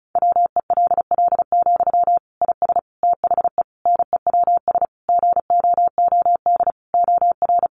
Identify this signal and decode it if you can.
Morse code, heard as a keyed tone, about 34 wpm, 700 Hz. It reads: WELL, IS THE NEWS GOOD OR
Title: Sentences from Top 800 Words 35wpm